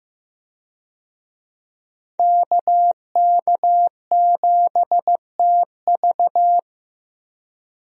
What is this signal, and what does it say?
Morse code, heard as a keyed tone, about 15 wpm, 700 Hz. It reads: KK7TV
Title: USA Call Signs - 5 Characters - Repeated 3x - 15wpm